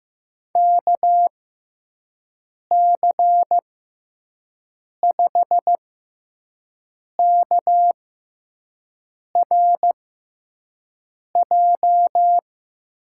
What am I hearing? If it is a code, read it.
Morse code, heard as a keyed tone, about 15 wpm, spaced as for 5 wpm, 700 Hz. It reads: KC5KRJ
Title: USA Call Signs - 6 Characters - 6x Character Spacing - 15wpm